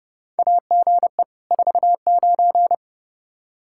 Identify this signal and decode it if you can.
Morse code, heard as a keyed tone, about 30 wpm, 700 Hz. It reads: AGE 49
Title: K1USN SST QSO Practice 30wpm